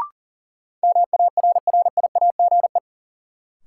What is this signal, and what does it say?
Morse code, heard as a keyed tone, about 40 wpm, 700 Hz. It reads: MARRIAGE
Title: Sentences from Top 800 Words - New Word Drill - 40wpm